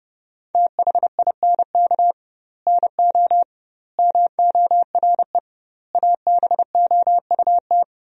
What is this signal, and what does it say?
Morse code, heard as a keyed tone, about 30 wpm, 700 Hz. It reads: THINK NO MORE ABOUT
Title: 2x Word-Spacing - Sentences from Top 200 Words - 30wpm